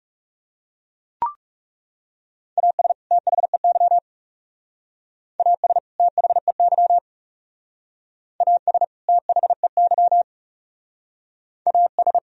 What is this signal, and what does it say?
Morse code, heard as a keyed tone, about 38 wpm, 700 Hz. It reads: AS THEY AS THEY AS THEY AS
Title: Speed-Racing - Sets of 2 Words 30wpm